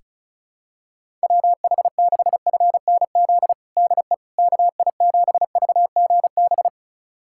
Morse code, heard as a keyed tone, about 36 wpm, 700 Hz. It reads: WH6FNZ DE KI7VGB
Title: USA Callsign DE Callsign - 6 Characters - 35wpm